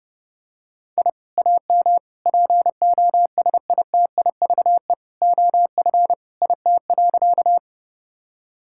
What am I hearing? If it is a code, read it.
Morse code, heard as a keyed tone, about 30 wpm, 700 Hz. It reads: I AM POSITIVE OF IT.
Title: Sentences from Top 1000 Words - Review Part 5 - 30wpm